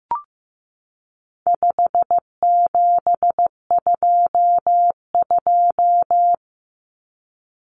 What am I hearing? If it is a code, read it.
Morse code, heard as a keyed tone, about 15 wpm, 700 Hz. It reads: 5722
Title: Common Cut Numbers - 4-Digit 15wpm